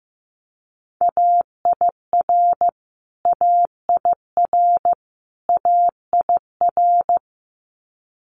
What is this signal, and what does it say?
Morse code, heard as a keyed tone, about 15 wpm, 700 Hz. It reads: AIR AIR AIR
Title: Top 200 Words Repeated 15wpm